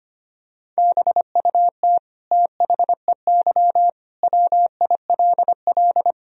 Morse code, heard as a keyed tone, about 25 wpm, 700 Hz. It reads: BUT THEY WILL
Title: Sentences from Top 1000 Words - Review Part 1 - 25wpm